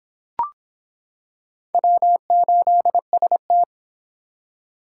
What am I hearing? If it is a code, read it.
Morse code, heard as a keyed tone, about 26 wpm, 700 Hz. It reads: W8ST